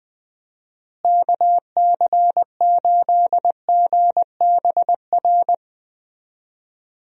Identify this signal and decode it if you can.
Morse code, heard as a keyed tone, about 20 wpm, 700 Hz. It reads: KC8GBR
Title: USA Call Signs - 6 Characters - Repeated 3x - 20wpm